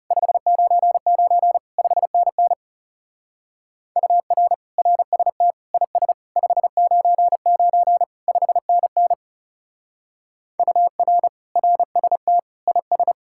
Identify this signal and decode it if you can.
Morse code, heard as a keyed tone, about 37 wpm, 700 Hz. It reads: 599 5NN UR RST IS 599 5NN UR RST IS
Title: Speed-Racing Common QSO Phrases - 30wpm